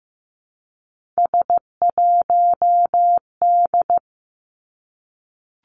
Morse code, heard as a keyed tone, about 15 wpm, 700 Hz. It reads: S1D
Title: Three Challenging Letter-Number Combinations 15wpm